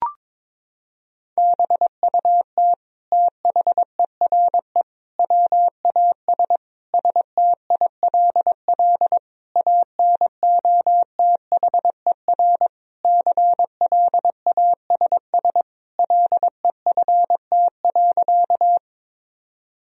Morse code, heard as a keyed tone, about 22 wpm, 700 Hz. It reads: BUT THERE WAS STILL ANOTHER CLASS LEFT.
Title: Sentences from Top 1000 Words - Review Part 2 - 22wpm